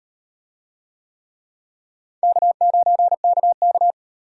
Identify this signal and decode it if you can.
Morse code, heard as a keyed tone, about 38 wpm, 700 Hz. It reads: K9KK